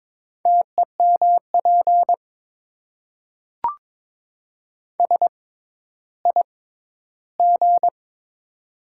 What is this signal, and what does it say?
Morse code, heard as a keyed tone, about 22 wpm, spaced as for 12 wpm, 700 Hz. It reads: TEMP SIG
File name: Top 90 QSO Elements - 6x Character Spacing - 22wpm